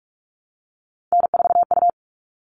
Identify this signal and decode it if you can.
Morse code, heard as a keyed tone, about 44 wpm, 700 Hz. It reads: N4U
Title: All Cut Numbers - 3-Digit 45wpm